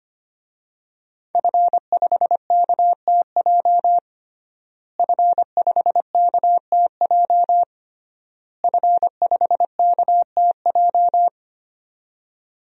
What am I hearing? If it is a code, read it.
Morse code, heard as a keyed tone, about 25 wpm, 700 Hz. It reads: F5KTJ F5KTJ F5KTJ
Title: DXCC-100 Call Signs - 5 Characters - Repeated 3x - 25wpm